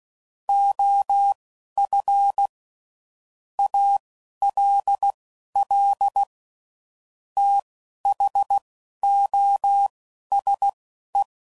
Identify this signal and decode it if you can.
Morse code, heard as a keyed tone, about 16 wpm, spaced as for 11 wpm, 800 Hz. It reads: OF ALL THOSE
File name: CW Code Groups @10WPM / Technical Subjects / AND Gate